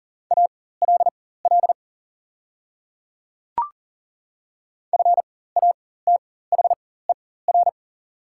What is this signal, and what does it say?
Morse code, heard as a keyed tone, about 40 wpm, spaced as for 19 wpm, 700 Hz. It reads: ALL FATHER
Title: 4x Character Spacing - Top 300 Words - 40wpm